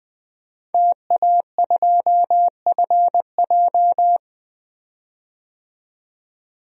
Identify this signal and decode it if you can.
Morse code, heard as a keyed tone, about 20 wpm, 700 Hz. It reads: TA2FJ
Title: DXCC-100 Call Signs - 5 Characters - 20wpm